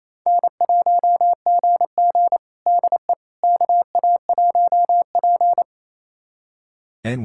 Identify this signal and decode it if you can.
Morse code, heard as a keyed tone, about 28 wpm, 690 Hz. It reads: N1GG DE KA1P
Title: USA Callsign DE Callsign - 4 Characters - 28wpm